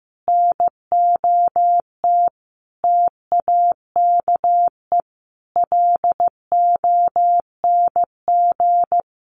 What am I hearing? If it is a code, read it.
Morse code, heard as a keyed tone, about 15 wpm, 700 Hz. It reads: NOT TAKE LONG
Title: Sets of 5 Words Encore2 15wpm